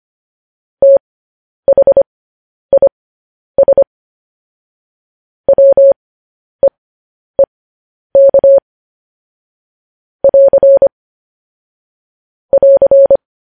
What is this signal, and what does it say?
Morse code, heard as a keyed tone, about 25 wpm, spaced as for 10 wpm, 550 Hz. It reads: THIS WEEK <AR> <AR>
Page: News Headlines in Morse Code at 10 WPM